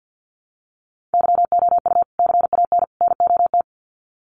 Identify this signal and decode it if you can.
Morse code, heard as a keyed tone, about 50 wpm, 700 Hz. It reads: YOU CAN NOT
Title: Sets of 3 Words 50wpm